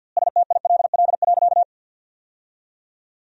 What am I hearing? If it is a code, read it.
Morse code, heard as a keyed tone, about 50 wpm, 700 Hz. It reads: STILL.
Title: 2x Word-Spacing - Sentences from Top 400 Words - 50wpm